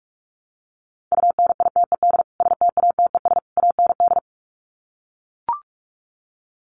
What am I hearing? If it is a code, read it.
Morse code, heard as a keyed tone, about 44 wpm, 700 Hz. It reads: UNITED STATES AND
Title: Sets of 3 Words 45wpm